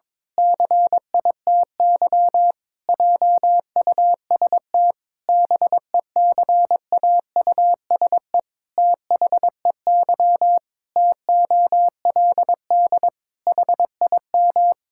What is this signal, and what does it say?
Morse code, heard as a keyed tone, about 21 wpm, 700 Hz. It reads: CITY JUST BECAUSE THEY TOLD HIM
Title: Pangram Challenge v2 - 22wpm